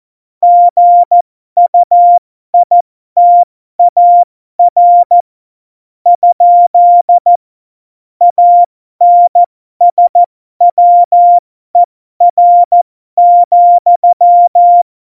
Morse code, heard as a keyed tone, about 14 wpm, spaced as for 12 wpm, 700 Hz. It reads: GUITAR ? ANSWER,
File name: Morsekurs / Witze